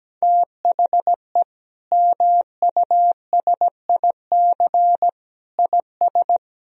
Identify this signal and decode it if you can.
Morse code, heard as a keyed tone, about 17 wpm, 700 Hz. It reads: THE MUSIC IS
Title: Sentences from Top 1000 Words - Review Part 1 - 17wpm